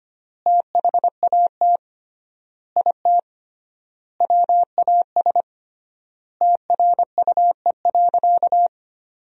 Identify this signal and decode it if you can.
Morse code, heard as a keyed tone, about 25 wpm, 700 Hz. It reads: THAT IT WAS TRUE.
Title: 3x Word-Spacing - Sentences from Top 400 Words - 25wpm